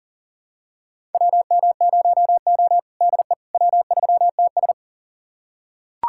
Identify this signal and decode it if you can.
Morse code, heard as a keyed tone, about 40 wpm, 700 Hz. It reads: WM0O DE W3TS
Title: USA Callsign DE Callsign - 4 Characters - 40wpm